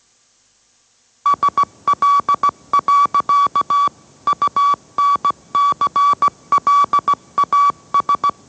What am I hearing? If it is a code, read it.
Morse code, heard as a keyed tone, about 17 wpm, 1140 Hz. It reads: SL. UNCLAS
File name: mrx / mrx080 18wpm